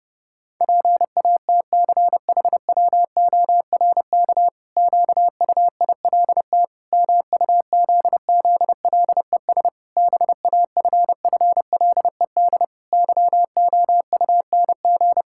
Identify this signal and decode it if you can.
Morse code, heard as a keyed tone, about 30 wpm, 700 Hz. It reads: PATCHWORK QUILT MUZZLES BAFFLED YOUNG
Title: Pangram Challenge v2 - 30wpm